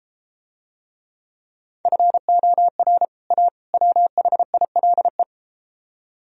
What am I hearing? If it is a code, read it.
Morse code, heard as a keyed tone, about 32 wpm, 700 Hz. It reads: FOR A WHILE